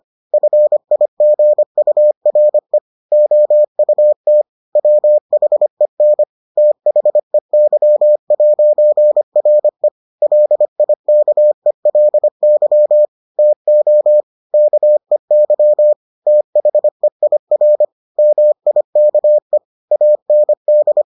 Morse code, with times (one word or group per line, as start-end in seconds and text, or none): 0.3-2.8 FIGURE
3.1-4.4 OUT
4.8-6.2 WHEN
6.6-9.9 THEY'RE
10.2-13.1 LIKELY
13.4-14.2 TO
14.5-15.9 KEY
16.3-17.9 THEIR
18.2-19.6 MIKE
19.9-21.0 AND